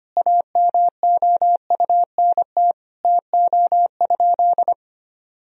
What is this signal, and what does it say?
Morse code, heard as a keyed tone, about 25 wpm, 700 Hz. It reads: AMOUNT TO?
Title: Sentences from Top 1000 Words - Review Part 4 - 25wpm